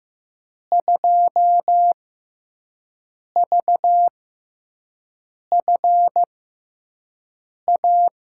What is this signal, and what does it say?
Morse code, heard as a keyed tone, about 15 wpm, spaced as for 5 wpm, 700 Hz. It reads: 2VFA
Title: USA Call Signs - 6x Character Spacing - 15wpm